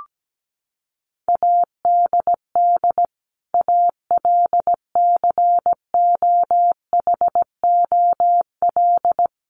Morse code, heard as a keyed tone, about 17 wpm, 700 Hz. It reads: ADD ALCOHOL
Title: Sentences from Top 1500 Words 17wpm